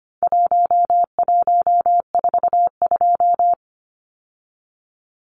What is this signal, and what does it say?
Morse code, heard as a keyed tone, about 25 wpm, 700 Hz. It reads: 1142
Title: Numbers - Contest Style - 4-Digit 25wpm